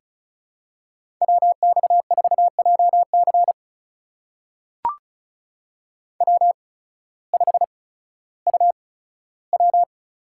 Morse code, heard as a keyed tone, about 34 wpm, spaced as for 16 wpm, 700 Hz. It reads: WX4JC W5UW
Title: USA Call Signs - 5 Characters - 8x Character Spacing - 35wpm